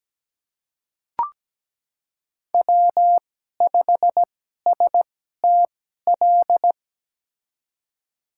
Morse code, heard as a keyed tone, about 17 wpm, spaced as for 12 wpm, 700 Hz. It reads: W5STL
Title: USA Call Signs - 5 Characters - 2x Character Spacing - 17wpm